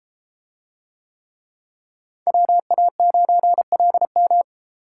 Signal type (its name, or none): Morse code